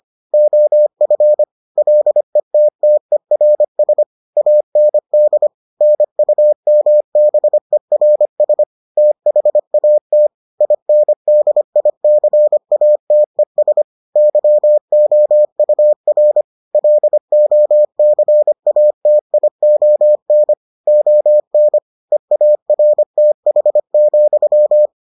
Morse code, heard as a keyed tone, about 25 wpm, 600 Hz. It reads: OF LETTERS AND NUMBERS THAT INDICATES YOUR LOCATION ON EARTH,